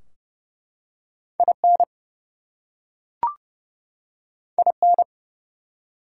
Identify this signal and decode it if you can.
Morse code, heard as a keyed tone, about 30 wpm, 700 Hz. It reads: IN IN
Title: Two Letter Words - 30wpm